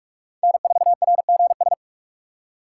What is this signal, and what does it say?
Morse code, heard as a keyed tone, about 46 wpm, 700 Hz. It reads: N4RGS
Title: USA Call Signs - 5 Characters - 2x Character Spacing - 45wpm